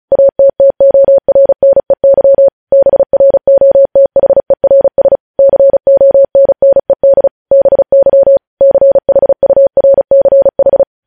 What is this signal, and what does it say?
Morse code, heard as a keyed tone, about 36 wpm, 550 Hz. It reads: ATTORNEY BROTHERS CONNED BY CHURCH
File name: podcast / News headlines-1x-35x35